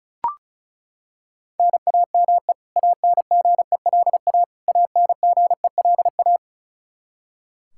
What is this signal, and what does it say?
Morse code, heard as a keyed tone, about 34 wpm, 700 Hz. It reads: NAME ANGELA ANGELA
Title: Practice QSO - Exchange 1 - Beginning - 35wpm